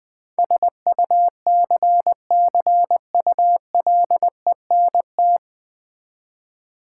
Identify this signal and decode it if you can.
Morse code, heard as a keyed tone, about 20 wpm, 700 Hz. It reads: SUCCULENT
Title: Difficult Words with Hints - 20wpm